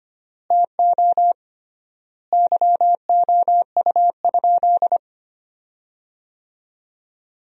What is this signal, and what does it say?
Morse code, heard as a keyed tone, about 25 wpm, 700 Hz. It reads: TO YOU?
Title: 3x Word-Spacing - Sentences from Top 100 Words - 25wpm